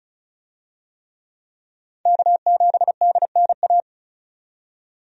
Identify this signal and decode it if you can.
Morse code, heard as a keyed tone, about 36 wpm, 700 Hz. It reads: K7DNA
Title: USA Call Signs - 5 Characters - 35wpm